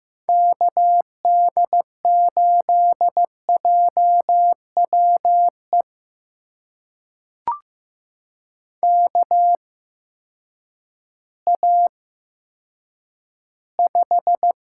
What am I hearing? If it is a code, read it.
Morse code, heard as a keyed tone, about 15 wpm, spaced as for 9 wpm, 700 Hz. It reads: KD8JWE KA5
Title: USA Call Signs - 8x Character Spacing - 15wpm